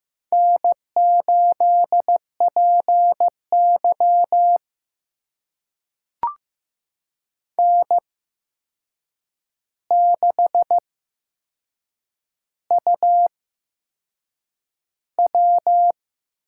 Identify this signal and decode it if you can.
Morse code, heard as a keyed tone, about 15 wpm, spaced as for 6 wpm, 700 Hz. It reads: N8PY N6UW
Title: USA Call Signs - 4 Characters - 8x Character Spacing - 15wpm